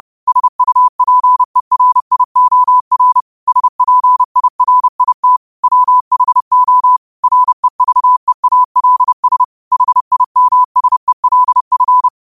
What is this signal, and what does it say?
Morse code, heard as a keyed tone, about 30 wpm, 1000 Hz. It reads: SUPERIOR SPIRIT WHO REVEALS HIMSELF
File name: qotdfiles / 30wpm